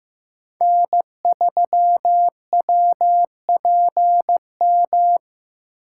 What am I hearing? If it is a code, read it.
Morse code, heard as a keyed tone, about 15 wpm, 700 Hz. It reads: N3WPM
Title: USA Call Signs - 5 Characters - 6x Character Spacing - 15wpm